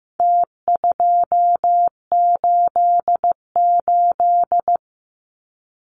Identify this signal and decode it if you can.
Morse code, heard as a keyed tone, about 15 wpm, 700 Hz. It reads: T288
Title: Numbers - Contest Style - 4-Digit 15wpm